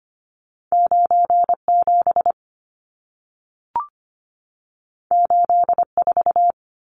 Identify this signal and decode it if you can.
Morse code, heard as a keyed tone, about 25 wpm, 700 Hz. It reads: 97 84
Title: Numbers - 2-Digit 25wpm